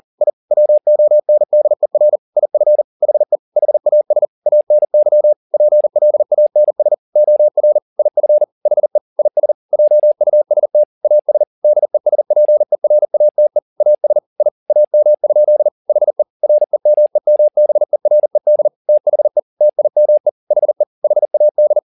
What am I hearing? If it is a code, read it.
Morse code, heard as a keyed tone, about 40 wpm, 600 Hz. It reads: I WONDER IF HE HAS ANY PLANS OR IF HE IS JUST AS DESPERATE AS I AM? HE REMEMBERED THE TIME HE HAD